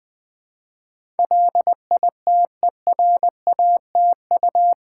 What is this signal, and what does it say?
Morse code, heard as a keyed tone, about 20 wpm, 700 Hz. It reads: LITERATU
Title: Build A Word - 4 Letters At A Time - 10-Letter Words - 20wpm